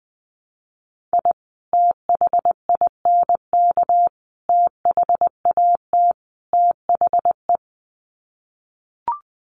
Morse code, heard as a keyed tone, about 20 wpm, 700 Hz. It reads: I THINK THAT THE
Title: Sets of 4 Words 20wpm